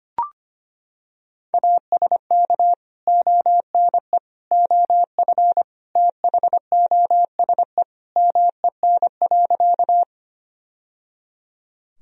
Morse code, heard as a keyed tone, about 25 wpm, 700 Hz. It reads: ASK ONE OF THOSE MEN.
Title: Sentences from Top 1000 Words - Review Part 1 - 25wpm